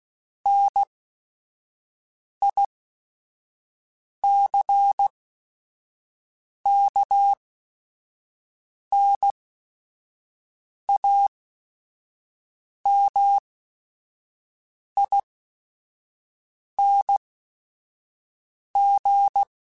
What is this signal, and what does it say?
Morse code, heard as a keyed tone, about 16 wpm, spaced as for 5 wpm, 800 Hz. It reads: NICKNAMING